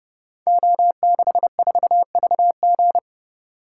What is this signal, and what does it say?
Morse code, heard as a keyed tone, about 30 wpm, 700 Hz. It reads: O64VG
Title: Code Groups - 5 Letter:Numbers - 30wpm